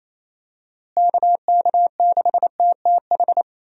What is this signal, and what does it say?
Morse code, heard as a keyed tone, about 28 wpm, 700 Hz. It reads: KK6TTH
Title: USA Call Signs - 6 Characters - Repeated 2x - 28wpm